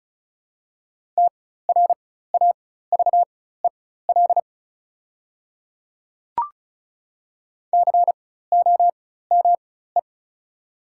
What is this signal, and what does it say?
Morse code, heard as a keyed tone, about 36 wpm, spaced as for 16 wpm, 700 Hz. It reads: TRAVEL COME
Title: 4x Character Spacing - Top 500 Words - 35wpm